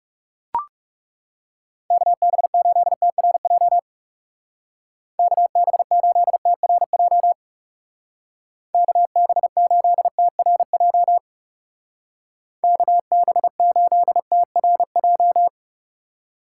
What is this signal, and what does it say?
Morse code, heard as a keyed tone, about 38 wpm, 700 Hz. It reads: KB8TRJ KB8TRJ KB8TRJ KB8TRJ